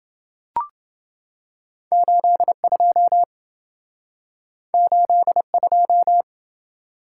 Morse code, heard as a keyed tone, about 27 wpm, 700 Hz. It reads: 82 82